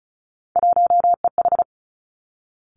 Morse code, heard as a keyed tone, about 36 wpm, 700 Hz. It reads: 1EH
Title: Three Challenging Letter-Number Combinations 35wpm